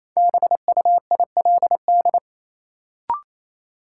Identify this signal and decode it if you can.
Morse code, heard as a keyed tone, about 28 wpm, 700 Hz. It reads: BUILD
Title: Sentences from Top 1000 Words - Review Part 1 - New Word Drill - 28wpm